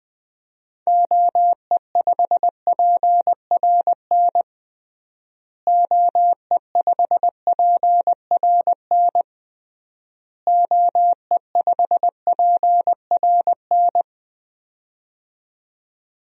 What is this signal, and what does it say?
Morse code, heard as a keyed tone, about 20 wpm, 700 Hz. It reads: OE5PRN OE5PRN OE5PRN
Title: DXCC-100 Call Signs - 6 Characters - Repeated 3x - 20wpm